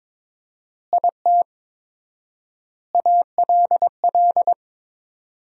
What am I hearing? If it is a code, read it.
Morse code, heard as a keyed tone, about 22 wpm, 700 Hz. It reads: IT ALL
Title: 4x Word-Spacing - Sentences from Top 500 Words - 22wpm